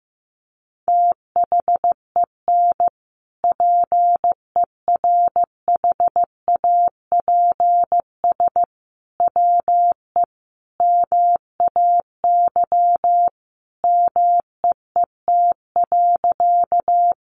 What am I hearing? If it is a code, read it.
Morse code, heard as a keyed tone, about 15 wpm, 700 Hz. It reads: THEN PERHAPS WE MAY MEET.
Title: Sentences from Top 1100 Words 15wpm